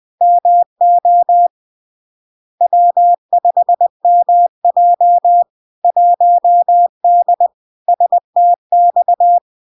Morse code, heard as a keyed tone, about 20 wpm, 700 Hz. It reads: MO W5MJ 1D STX
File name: LARK-FD-2022-Log-in-CW-20wpm-700-Hz